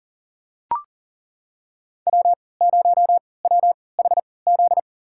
Morse code, heard as a keyed tone, about 40 wpm, spaced as for 23 wpm, 700 Hz. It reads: W0WHZ
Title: USA Call Signs - 5 Characters - 3x Character Spacing - 40wpm